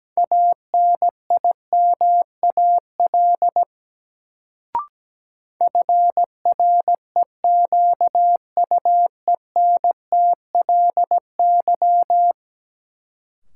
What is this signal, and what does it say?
Morse code, heard as a keyed tone, about 17 wpm, 700 Hz. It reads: ANIMAL FREQUENTLY
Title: Sentences from Top 1300 Words - New Word Drill Alt - 17wpm